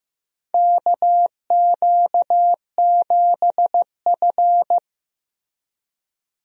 Morse code, heard as a keyed tone, about 15 wpm, 700 Hz. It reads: KQ7F
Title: USA Call Signs - 4 Characters - Repeated 2x - 15wpm